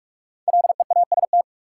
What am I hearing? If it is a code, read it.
Morse code, heard as a keyed tone, about 44 wpm, 700 Hz. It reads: LEAST
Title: Sets of 2 Words - New Word Drill - 45wpm